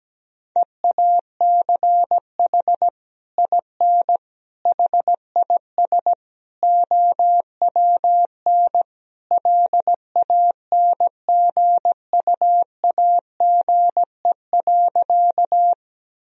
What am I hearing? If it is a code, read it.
Morse code, heard as a keyed tone, about 17 wpm, 700 Hz. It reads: EACH IN HIS OWN LANGUAGE.
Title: Sentences from Top 1000 Words - Review Part 3 - 17wpm